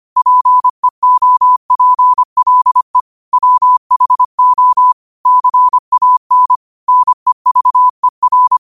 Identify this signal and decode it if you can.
Morse code, heard as a keyed tone, about 25 wpm, 1000 Hz. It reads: PEOPLE WHO CAN NEVER